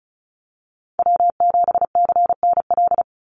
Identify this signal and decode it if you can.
Morse code, heard as a keyed tone, about 34 wpm, 700 Hz. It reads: W7CNL
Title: NA SOTA Chaser Callsigns 35wpm